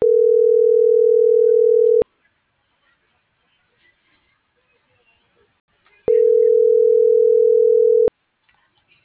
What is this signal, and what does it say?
Ringback tone (North America), 440 + 480 Hz.